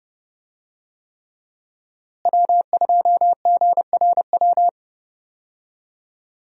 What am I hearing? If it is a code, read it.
Morse code, heard as a keyed tone, about 30 wpm, 700 Hz. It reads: W2GRW
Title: USA Call Signs - 5 Characters - Repeated 3x - 30wpm